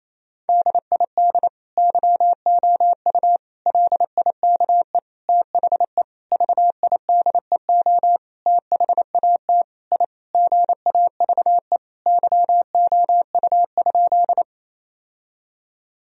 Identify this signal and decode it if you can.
Morse code, heard as a keyed tone, about 27 wpm, 700 Hz. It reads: DID YOU LIKE THE VIDEO THAT I GAVE YOU?
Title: Sentences from Top 1000 Words - Review Part 4 - 28wpm